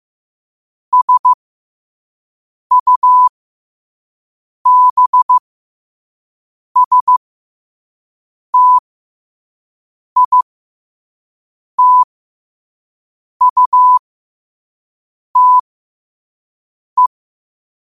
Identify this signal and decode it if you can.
Morse code, heard as a keyed tone, about 15 wpm, spaced as for 5 wpm, 1000 Hz. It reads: SUBSTITUTE
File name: qotdfiles / 5wpm